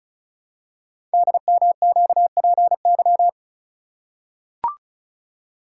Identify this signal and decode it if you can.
Morse code, heard as a keyed tone, about 36 wpm, 690 Hz. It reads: DMQPY
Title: Code Groups - 5 Letters - 35wpm